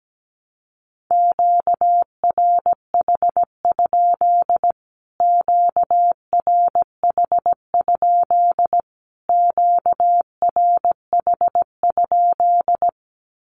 Morse code, heard as a keyed tone, about 17 wpm, 700 Hz. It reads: QRH? QRH? QRH?
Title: Q-Codes Repeated 3x 17wpm